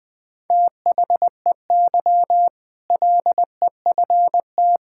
Morse code, heard as a keyed tone, about 21 wpm, 700 Hz. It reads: THEY LEFT
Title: Sentences with 2-Word Phrases - 20wpm